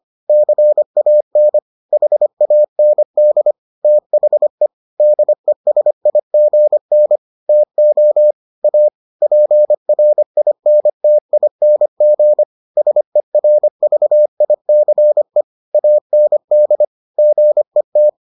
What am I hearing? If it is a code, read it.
Morse code, heard as a keyed tone, about 25 wpm, 600 Hz. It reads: CAN HAND THE DESIGN TO A PRINTING SERVICE AND GET